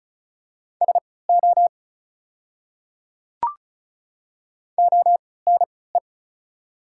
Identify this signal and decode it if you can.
Morse code, heard as a keyed tone, about 36 wpm, spaced as for 20 wpm, 690 Hz. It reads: SO ONE
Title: 3x Character Spacing - Top 100 Words - 35wpm